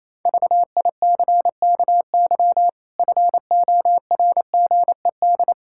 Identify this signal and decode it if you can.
Morse code, heard as a keyed tone, about 28 wpm, 700 Hz. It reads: VICKY FORGED
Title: Pangram Challenge v2 - 28wpm